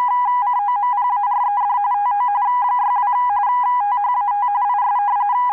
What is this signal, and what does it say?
RTTY, about 45.45 baud. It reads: COMPARED TO 62 THAT IS